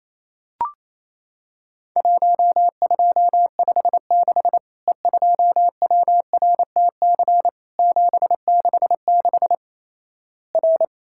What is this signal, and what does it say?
Morse code, heard as a keyed tone, about 28 wpm, 700 Hz. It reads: 1256 E2WRTC 766 R
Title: Worked All Europe DX Contest - QTC Exchange - 28wpm